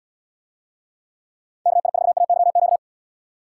Morse code, heard as a keyed tone, about 71 wpm, 700 Hz. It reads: KI4AYQ